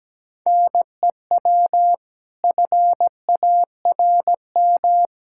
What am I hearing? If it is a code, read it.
Morse code, heard as a keyed tone, about 17 wpm, 700 Hz. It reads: NEW FARM
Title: Pangram Challenge v2 - 17wpm